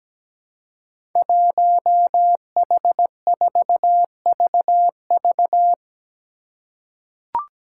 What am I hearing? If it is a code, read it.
Morse code, heard as a keyed tone, about 17 wpm, 700 Hz. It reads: 1H4VV
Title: Five Challenging Letter-Number Combinations 17wpm